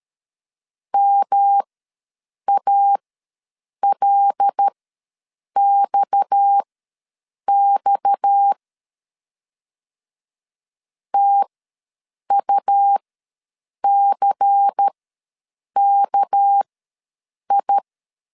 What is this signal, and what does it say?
Morse code, heard as a keyed tone, about 13 wpm, spaced as for 7 wpm, 800 Hz. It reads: MALXX TUCKI
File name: LEZIONI / 6 PKX0 / 6 PKX ws 12wpm 3106